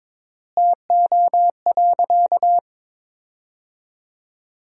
Morse code, heard as a keyed tone, about 22 wpm, 700 Hz. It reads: TO.
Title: 3x Word-Spacing - Sentences from Top 400 Words - 22wpm